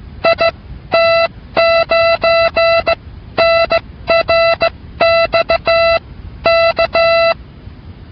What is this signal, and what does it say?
Morse code, heard as a keyed tone, about 16 wpm, spaced as for 13 wpm, 670 Hz. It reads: IT9NRX K